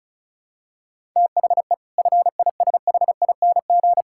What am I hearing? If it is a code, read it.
Morse code, heard as a keyed tone, about 36 wpm, 700 Hz. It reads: THE FISHING
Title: Sentences from Top 1000 Words - Review Part 6 - 35wpm